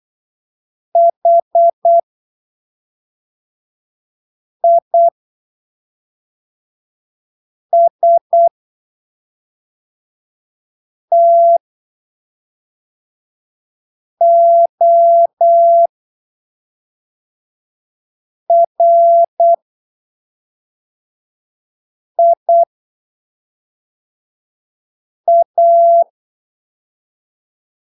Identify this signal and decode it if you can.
Morse code, heard as a keyed tone, about 8 wpm, spaced as for 3 wpm, 680 Hz. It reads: HISTORIA
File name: arquivos / texto / texto 05 08ppm